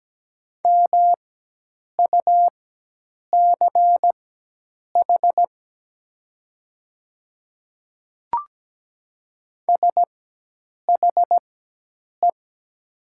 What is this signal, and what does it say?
Morse code, heard as a keyed tone, about 17 wpm, spaced as for 8 wpm, 700 Hz. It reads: MUCH SHE
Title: 4x Character Spacing - Top 300 Words - 17wpm